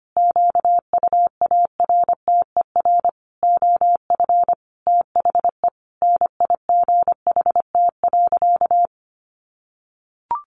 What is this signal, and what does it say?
Morse code, heard as a keyed tone, about 25 wpm, 700 Hz. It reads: QUARTER OF THE NIGHT.
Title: Sentences from Top 1300 Words 25wpm